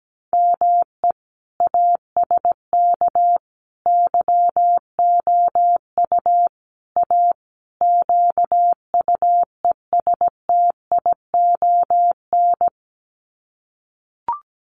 Morse code, heard as a keyed tone, about 17 wpm, 700 Hz. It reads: ME ASK YOU A QUESTION
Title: Sets of 5 Words Encore2 17wpm